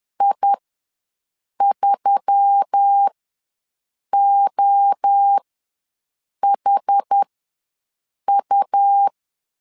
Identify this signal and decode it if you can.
Morse code, heard as a keyed tone, about 10 wpm, spaced as for 6 wpm, 800 Hz. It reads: I3OHU
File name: LEZIONI / 5 LBVC / 5 LB ws 10wpm 3616